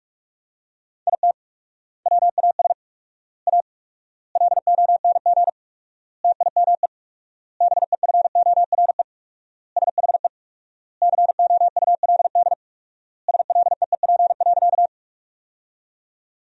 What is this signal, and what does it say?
Morse code, heard as a keyed tone, about 46 wpm, 700 Hz. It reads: IT WAS A LONG TIME BEFORE SHE COULD SLEEP.
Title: 4x Word-Spacing - Sentences from Top 500 Words - 45wpm